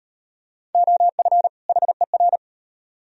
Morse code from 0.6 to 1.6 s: OF

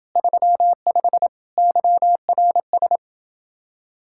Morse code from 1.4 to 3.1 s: YRS

